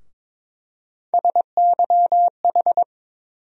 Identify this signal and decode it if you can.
Morse code, heard as a keyed tone, about 22 wpm, 700 Hz. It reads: SYH
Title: USA Call Sign Postfixes - 3 Letters 22wpm